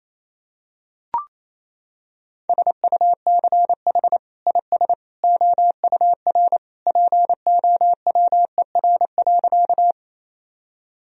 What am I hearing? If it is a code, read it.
Morse code, heard as a keyed tone, about 28 wpm, 700 Hz. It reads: SUCH IS OUR POWER.
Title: Sentences from Top 1000 Words - Review Part 2 - 28wpm